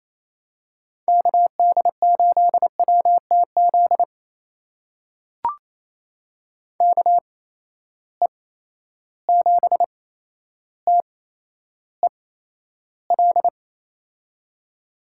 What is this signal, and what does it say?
Morse code, heard as a keyed tone, about 28 wpm, spaced as for 12 wpm, 700 Hz. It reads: KD8WTZ KE7TEL
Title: USA Call Signs - 8x Character Spacing - 28wpm